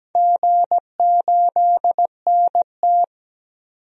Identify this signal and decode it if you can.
Morse code, heard as a keyed tone, about 17 wpm, 700 Hz. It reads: G8NT
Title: DXCC-100 Call Signs - 4 Characters - 17wpm